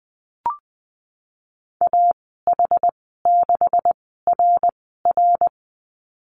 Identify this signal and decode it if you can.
Morse code, heard as a keyed tone, about 20 wpm, spaced as for 15 wpm, 700 Hz. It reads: AH6RR
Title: 128 Lesson 21g - Number 6 - Call Signs 20wpm